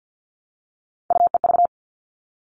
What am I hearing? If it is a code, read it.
Morse code, heard as a keyed tone, about 50 wpm, 700 Hz. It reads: UEV
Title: All Cut Numbers - 3-Digit 50wpm